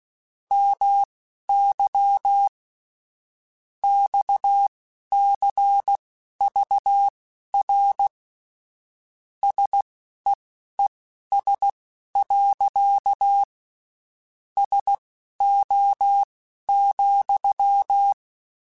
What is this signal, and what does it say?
Morse code, heard as a keyed tone, about 16 wpm, spaced as for 12 wpm, 800 Hz. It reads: MY XCVR SEES. SO,
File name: (spaced) / QSOs Spaced-mp3s / WD0FBT 1 @15 sp